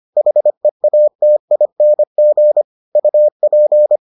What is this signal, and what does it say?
Morse code, heard as a keyed tone, about 25 wpm, 600 Hz. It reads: HEATING UP